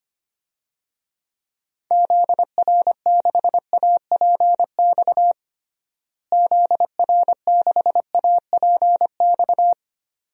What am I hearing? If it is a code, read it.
Morse code, heard as a keyed tone, about 25 wpm, 700 Hz. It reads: ZR6APX ZR6APX
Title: DXCC-100 Call Signs - 6 Characters - Repeated 2x - 25wpm